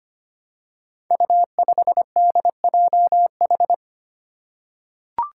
Morse code, heard as a keyed tone, about 25 wpm, 700 Hz. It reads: U5DJH
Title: Five Challenging Letter-Number Combinations 25wpm